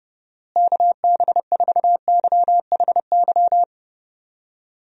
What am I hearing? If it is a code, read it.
Morse code, heard as a keyed tone, about 30 wpm, 700 Hz. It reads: KB4YHY
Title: USA Call Signs - 6 Characters - Repeated 3x - 30wpm